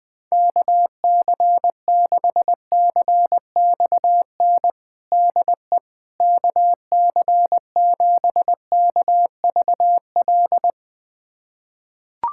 Morse code, heard as a keyed tone, about 21 wpm, 700 Hz. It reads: KC6CXN DE KC7KVL
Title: USA Callsign DE Callsign - 6 Characters - 20wpm